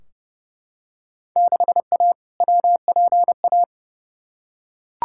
Morse code, heard as a keyed tone, about 31 wpm, 700 Hz. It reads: 6A WPA
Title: Field Day Exchanges Encore - 30wpm